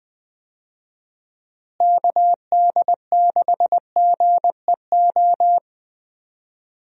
Morse code, heard as a keyed tone, about 20 wpm, 700 Hz. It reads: KD6GEO
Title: USA Call Signs - 6 Characters - Repeated 2x - 20wpm